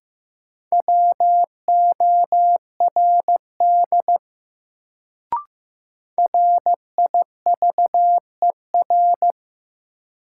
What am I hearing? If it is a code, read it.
Morse code, heard as a keyed tone, about 15 wpm, 700 Hz. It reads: WORD RIVER
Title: Sentences from Top 300 Words - New Word Drill - 15wpm